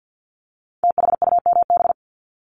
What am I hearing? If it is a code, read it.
Morse code, heard as a keyed tone, about 50 wpm, 700 Hz. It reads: THUMB
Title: Multi-letter Phonemes 50wpm